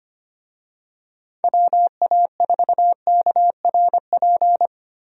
Morse code, heard as a keyed tone, about 25 wpm, 700 Hz. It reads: WA4KRP
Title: USA Call Signs - 6 Characters - Repeated 3x - 25wpm